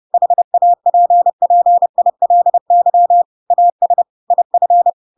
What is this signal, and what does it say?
Morse code, heard as a keyed tone, about 30 wpm, 700 Hz. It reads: HAPPILY AS IF
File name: Pinocchio / Pinocchio ch6 30 30 ik1hge